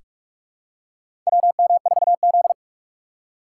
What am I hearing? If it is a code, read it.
Morse code, heard as a keyed tone, about 42 wpm, 700 Hz. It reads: WM4Z
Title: USA Call Signs - 4 Characters - 3x Character Spacing - 45wpm